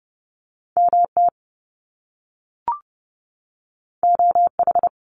Morse code, heard as a keyed tone, about 30 wpm, 700 Hz. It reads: MT OH
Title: US States 30wpm